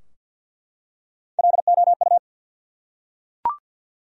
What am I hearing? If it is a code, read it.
Morse code, heard as a keyed tone, about 48 wpm, 700 Hz. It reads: LOU
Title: USA Common Names - Max 4 Letters - 50wpm